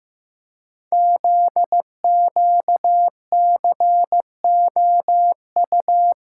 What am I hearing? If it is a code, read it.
Morse code, heard as a keyed tone, about 15 wpm, 700 Hz. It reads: ZQCOU
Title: Code Groups - 5 Letters - 15wpm